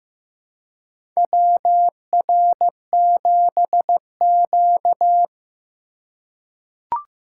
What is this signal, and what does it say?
Morse code, heard as a keyed tone, about 15 wpm, 700 Hz. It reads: WR7Q
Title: USA Call Signs - 4 Characters - 2x Character Spacing - 15wpm